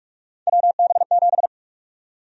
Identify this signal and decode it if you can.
Morse code, heard as a keyed tone, about 47 wpm, 700 Hz. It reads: WB7
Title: USA Call Sign Prefixes - Repeated 2x - 45wpm